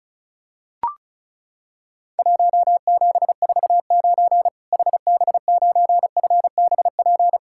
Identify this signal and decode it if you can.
Morse code, heard as a keyed tone, about 34 wpm, 700 Hz. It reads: 1749 HB9FBP
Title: Worked All Europe DX Contest - QTC Exchange - 35wpm